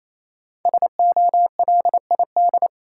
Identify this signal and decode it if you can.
Morse code, heard as a keyed tone, about 28 wpm, 700 Hz. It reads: SOLID
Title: Five Letter Words - 28wpm